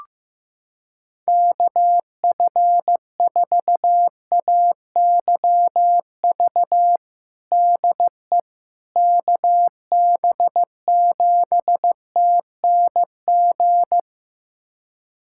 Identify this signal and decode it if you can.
Morse code, heard as a keyed tone, about 16 wpm, 700 Hz. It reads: KF4AYV DE KB7TNG
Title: USA Callsign DE Callsign - 6 Characters - 15wpm